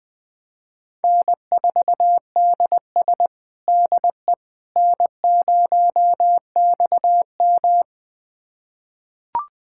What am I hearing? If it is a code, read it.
Morse code, heard as a keyed tone, about 20 wpm, 700 Hz. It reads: N4DS DE N0XM
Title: USA Callsign DE Callsign - 4 Characters - 20wpm